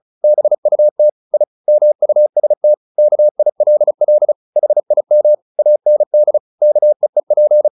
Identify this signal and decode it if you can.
Morse code, heard as a keyed tone, about 34 wpm, 600 Hz. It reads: BUT I MUST KILL HIM AND KEEP
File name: the old man and the sea / the old man and the sea 1 35wpm